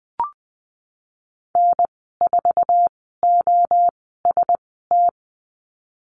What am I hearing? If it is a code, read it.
Morse code, heard as a keyed tone, about 19 wpm, 700 Hz. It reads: N 4 O S T
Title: 048 Lesson 9g - Number 4 - Call Signs 20wpm